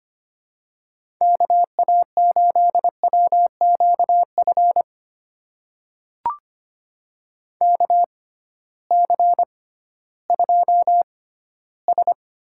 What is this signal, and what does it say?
Morse code, heard as a keyed tone, about 25 wpm, spaced as for 15 wpm, 700 Hz. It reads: KA8WQF KC2S
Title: USA Call Signs - 6 Characters - 6x Character Spacing - 25wpm